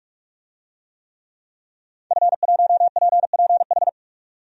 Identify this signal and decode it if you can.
Morse code, heard as a keyed tone, about 46 wpm, 700 Hz. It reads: F1PPH
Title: DXCC-100 Call Signs - 5 Characters - 45wpm